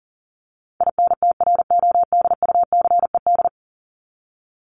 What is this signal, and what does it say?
Morse code, heard as a keyed tone, about 40 wpm, 700 Hz. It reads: INTRODUCED
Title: ED Suffix - Top 100 Words with Base Words 40wpm